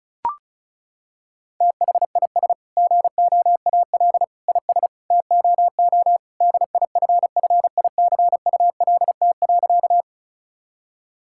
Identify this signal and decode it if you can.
Morse code, heard as a keyed tone, about 36 wpm, 700 Hz. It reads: THIS GOAL IS TOO DIFFICULT.
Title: Sentences from Top 1000 Words - Review Part 5 - 35wpm